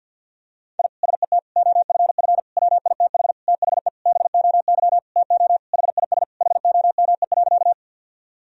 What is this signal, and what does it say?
Morse code, heard as a keyed tone, about 51 wpm, 700 Hz. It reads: I SET OFF WITH THE BOY TO HIS HOME.
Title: Sentences from Top 1000 Words - Review Part 1 - 50wpm